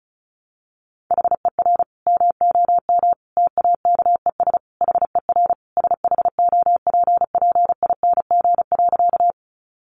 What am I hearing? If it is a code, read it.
Morse code, heard as a keyed tone, about 34 wpm, 700 Hz. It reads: HER MOM TAKES HER SHOPPING.
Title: Additional Easy Phrases 35wpm